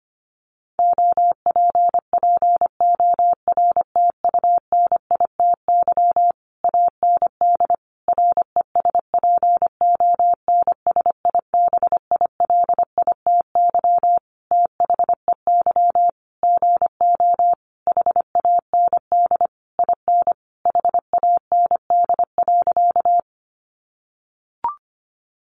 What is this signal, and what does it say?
Morse code, heard as a keyed tone, about 25 wpm, 700 Hz. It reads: OPPORTUNITY AND RESPONSIBILITY THEY GO HAND IN HAND.
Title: Sentences from Top 1200 Words 25wpm